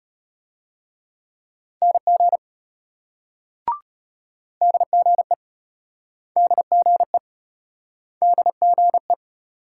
Morse code, heard as a keyed tone, about 33 wpm, 700 Hz. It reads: NG DGE DGE DGE